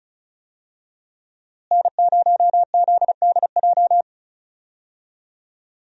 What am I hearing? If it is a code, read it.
Morse code, heard as a keyed tone, about 34 wpm, 700 Hz. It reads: N0ZDJ
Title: USA Call Signs - 5 Characters - Repeated 2x - 35wpm